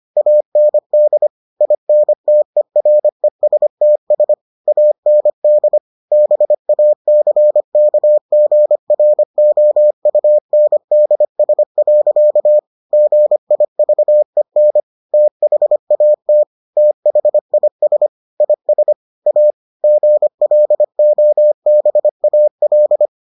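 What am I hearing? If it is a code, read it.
Morse code, heard as a keyed tone, about 25 wpm, 600 Hz. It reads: AND INTERESTS AND BACKGROUNDS. GIVEN THAT THIS IS A GLOBAL
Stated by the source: Morse edition of the Foundations of Amateur Radio episodes.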